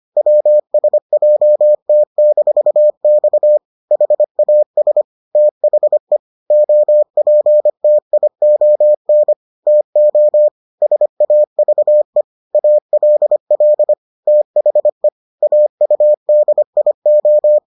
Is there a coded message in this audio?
Yes: Morse code from 0.2 to 17.6 s, heard as a keyed tone, about 25 wpm, 600 Hz. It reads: WSJT-X HAS THE OPTION TO SAVE ALL THE AUDIO